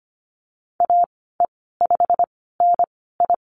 Morse code, heard as a keyed tone, about 25 wpm, spaced as for 16 wpm, 700 Hz. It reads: AE5NI
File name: Lesson 15g - Number 5 - Call Signs 25wpm